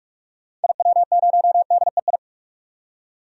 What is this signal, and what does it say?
Morse code, heard as a keyed tone, about 44 wpm, 700 Hz. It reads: IW0DEI
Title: DXCC-100 Call Signs - 6 Characters - 45wpm